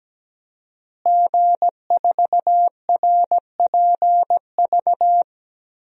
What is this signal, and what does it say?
Morse code, heard as a keyed tone, about 17 wpm, 700 Hz. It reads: G4RPV
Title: DXCC-100 Call Signs - 5 Characters - Repeated 3x - 17wpm